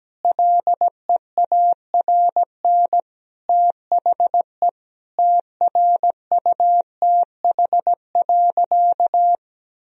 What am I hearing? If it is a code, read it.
Morse code, heard as a keyed tone, about 17 wpm, 700 Hz. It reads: LEARN THE TRUTH.
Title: Sentences from Top 1000 Words - Review Part 4 - 17wpm